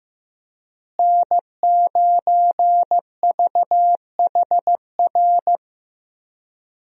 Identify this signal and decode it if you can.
Morse code, heard as a keyed tone, about 15 wpm, 700 Hz. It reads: N9VHR
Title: USA Call Signs - 5 Characters - Repeated 3x - 15wpm